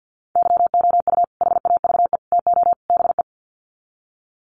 Morse code, heard as a keyed tone, about 50 wpm, 700 Hz. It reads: YOU HAVE TO BE
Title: Sets of 4 Words 50wpm